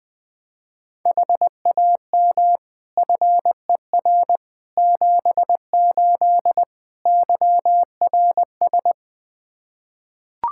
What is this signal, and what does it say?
Morse code, heard as a keyed tone, about 20 wpm, 700 Hz. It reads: HAM FER 78 YRS
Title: Practice QSO - Exchange 3 - More Info - 20wpm